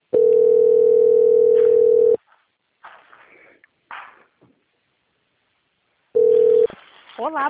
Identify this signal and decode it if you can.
Ringback tone (North America), 440 + 480 Hz.
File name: out- - - - - .19048